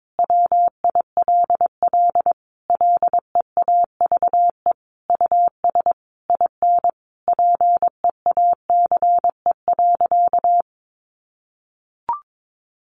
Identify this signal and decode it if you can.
Morse code, heard as a keyed tone, about 22 wpm, 700 Hz. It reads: WILL LEAVE US IN PEACE.
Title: Sentences from Top 1500 Words 22wpm